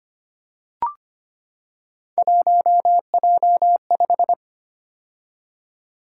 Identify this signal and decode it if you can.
Morse code, heard as a keyed tone, about 25 wpm, 700 Hz. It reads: 1J5
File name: Three Letter-Number Combinations - 25wpm